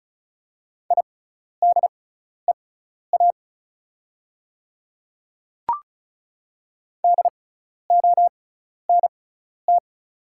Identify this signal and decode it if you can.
Morse code, heard as a keyed tone, about 36 wpm, spaced as for 12 wpm, 690 Hz. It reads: IDEA DONT
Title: 6x Character Spacing - Top 300 Words - 35wpm